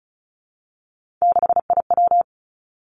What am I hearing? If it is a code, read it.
Morse code, heard as a keyed tone, about 36 wpm, 700 Hz. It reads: 6IW
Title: Three Challenging Letter-Number Combinations 35wpm